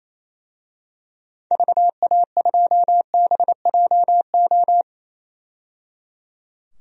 Morse code, heard as a keyed tone, about 28 wpm, 700 Hz. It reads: VA2BJO
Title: DXCC-100 Call Signs - 6 Characters - 28wpm